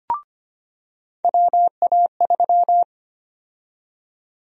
Morse code, heard as a keyed tone, about 25 wpm, 700 Hz. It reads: WA3
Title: USA Call Sign Prefixes - 25wpm